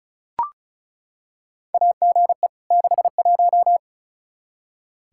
Morse code, heard as a keyed tone, about 34 wpm, 700 Hz. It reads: AGE 61
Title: Practice QSO - Exchange 3 - More Info - 35wpm